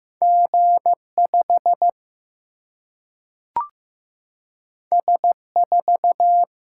Morse code, heard as a keyed tone, about 15 wpm, 700 Hz. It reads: G5 S4
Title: Single-Letter Single-Number 15wpm